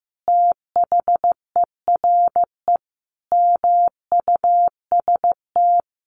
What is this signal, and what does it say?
Morse code, heard as a keyed tone, about 15 wpm, 700 Hz. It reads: THERE MUST
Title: Sentences from Top 1300 Words 15wpm